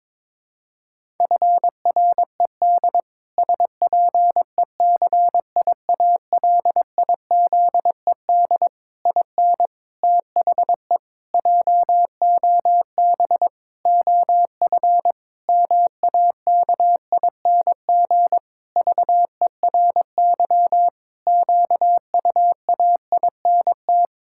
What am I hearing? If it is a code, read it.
Morse code, heard as a keyed tone, about 22 wpm, 700 Hz. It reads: FRED SPECIALIZED IN THE JOB OF MAKING VERY QUAINT
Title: Pangram Challenge v2 - 22wpm